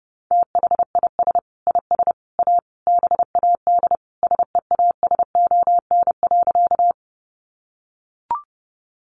Morse code, heard as a keyed tone, about 30 wpm, 700 Hz. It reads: THIS IS A BAD SEASON.
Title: Sentences from Top 1400 Words 30wpm